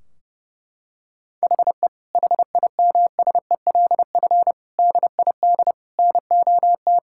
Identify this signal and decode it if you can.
Morse code, heard as a keyed tone, about 30 wpm, 700 Hz. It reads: HE HIMSELF DID NOT
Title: Sentences from Top 1600 Words 30wpm